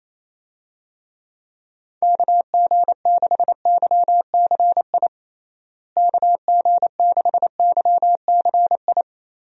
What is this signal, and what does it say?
Morse code, heard as a keyed tone, about 28 wpm, 700 Hz. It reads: KG6YCI KG6YCI
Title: USA Call Signs - 6 Characters - Repeated 3x - 28wpm